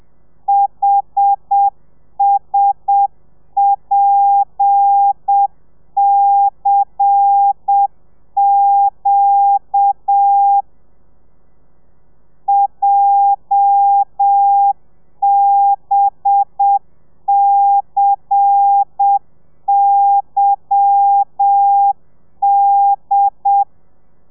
Morse code, heard as a keyed tone, about 7 wpm, 800 Hz. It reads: HSPCQ JBCYD
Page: Lezione 1: audio a 30 CPM